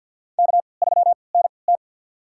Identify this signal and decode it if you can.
Morse code, heard as a keyed tone, about 51 wpm, spaced as for 28 wpm, 700 Hz. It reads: K3NT
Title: USA Call Signs - 3x Character Spacing - 50wpm